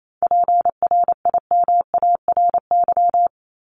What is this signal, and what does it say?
Morse code, heard as a keyed tone, about 28 wpm, 700 Hz. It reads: PRIMARY
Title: LY Suffix - Top 100 Words with Base Words 28wpm